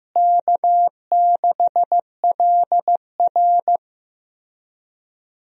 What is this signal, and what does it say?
Morse code, heard as a keyed tone, about 15 wpm, 700 Hz. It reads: K6LR
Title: USA Call Signs - 4 Characters - Repeated 3x - 15wpm